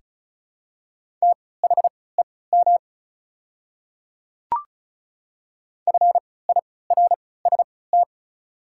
Morse code, heard as a keyed tone, about 36 wpm, spaced as for 20 wpm, 700 Hz. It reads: THEM FIRST
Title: 3x Character Spacing - Top 100 Words - 35wpm